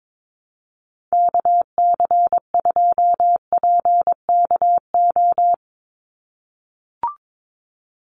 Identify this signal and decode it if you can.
Morse code, heard as a keyed tone, about 22 wpm, 700 Hz. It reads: KC2PKO
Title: USA Call Signs 22wpm